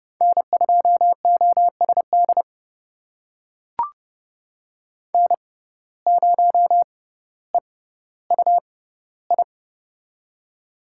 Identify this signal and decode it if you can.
Morse code, heard as a keyed tone, about 29 wpm, spaced as for 15 wpm, 700 Hz. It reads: N2OSD N0EUI
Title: USA Call Signs - 5 Characters - 6x Character Spacing - 30wpm